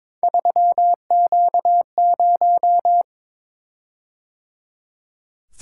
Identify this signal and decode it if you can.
Morse code, heard as a keyed tone, about 22 wpm, 700 Hz. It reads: 3Q0
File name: practice / 22wpm / Three Letter-Number Combinations - 22wpm